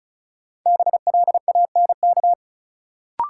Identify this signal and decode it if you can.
Morse code, heard as a keyed tone, about 36 wpm, 700 Hz. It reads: BLANK
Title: Sentences from Top 1500 Words - New Word Drill Alt - 35wpm